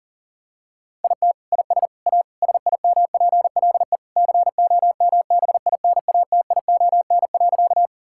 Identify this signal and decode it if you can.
Morse code, heard as a keyed tone, about 40 wpm, 700 Hz. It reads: IT IS A SIMPLE COMBINATION.
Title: Sentences from Top 1000 Words - Review Part 6 - 40wpm